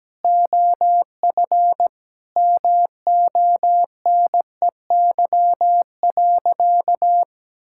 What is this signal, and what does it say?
Morse code, heard as a keyed tone, about 17 wpm, 700 Hz. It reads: OF MONEY.
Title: Sentences from Top 1000 Words - Review Part 5 - 17wpm